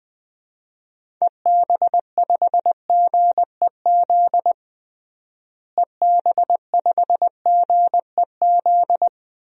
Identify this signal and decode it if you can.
Morse code, heard as a keyed tone, about 20 wpm, 700 Hz. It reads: EB5GEZ EB5GEZ
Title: DXCC-100 Call Signs - 6 Characters - Repeated 2x - 20wpm